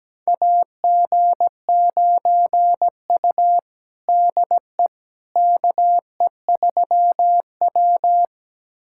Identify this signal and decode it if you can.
Morse code, heard as a keyed tone, about 17 wpm, 700 Hz. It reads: AG9U DE KE3W
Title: USA Callsign DE Callsign - 4 Characters - 17wpm